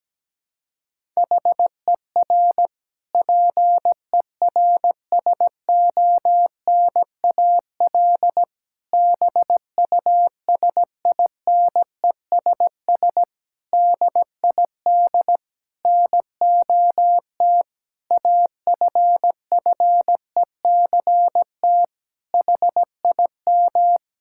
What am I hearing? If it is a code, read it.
Morse code, heard as a keyed tone, about 17 wpm, 700 Hz. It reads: HER PERSONAL BUSINESS DID NOT AFFECT HIM
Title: Sentences from Top 1000 Words - Review Part 6 - 17wpm